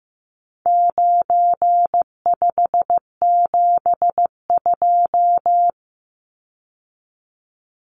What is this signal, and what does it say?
Morse code, heard as a keyed tone, about 15 wpm, 700 Hz. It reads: 9572
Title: Numbers - 4-Digit 15wpm